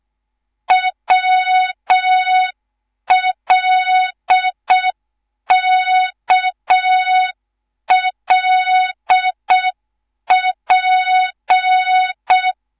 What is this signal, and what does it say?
Morse code, heard as a keyed tone, about 6 wpm, 3700 Hz. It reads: WLKLP
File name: Morze-HA2MN/hang/11 GYAKORLAT